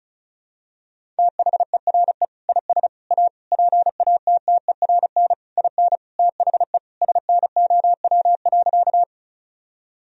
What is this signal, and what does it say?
Morse code, heard as a keyed tone, about 36 wpm, 700 Hz. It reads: THERE IS A PATTERN IN THE SNOW.
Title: Sentences from Top 1000 Words - Review Part 2 - 35wpm